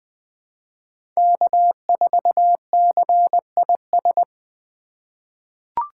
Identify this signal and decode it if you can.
Morse code, heard as a keyed tone, about 20 wpm, 700 Hz. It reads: K4CIS
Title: USA Call Signs - 2x Character Spacing - 20wpm